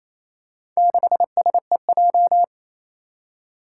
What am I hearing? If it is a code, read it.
Morse code, heard as a keyed tone, about 28 wpm, 700 Hz. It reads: 6SEJ
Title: Four Challenging Letter-Number Combinations 28wpm